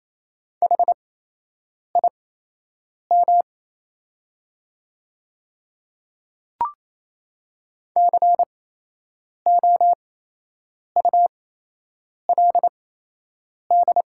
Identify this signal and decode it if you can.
Morse code, heard as a keyed tone, about 27 wpm, spaced as for 8 wpm, 700 Hz. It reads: HIM COULD
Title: 8x Character Spacing - Top 100 Words - 28wpm